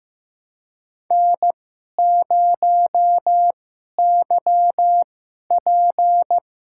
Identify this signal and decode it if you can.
Morse code, heard as a keyed tone, about 15 wpm, spaced as for 11 wpm, 700 Hz. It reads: N0YP
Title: USA Call Signs - 2x Character Spacing - 15wpm